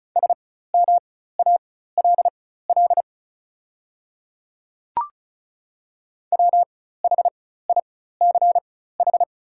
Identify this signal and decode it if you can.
Morse code, heard as a keyed tone, about 36 wpm, spaced as for 16 wpm, 700 Hz. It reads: SMALL WHICH
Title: 4x Character Spacing - Top 300 Words - 35wpm